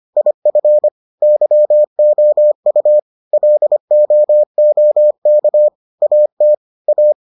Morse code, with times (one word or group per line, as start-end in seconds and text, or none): 0.2-0.9 IF
1.2-3.0 YOU
3.3-5.7 LOOK
6.0-6.5 AT
6.9-7.1 A